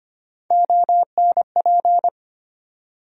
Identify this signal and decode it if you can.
Morse code, heard as a keyed tone, about 25 wpm, 700 Hz. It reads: ONP
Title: USA Call Sign Postfixes - 3 Letters 25wpm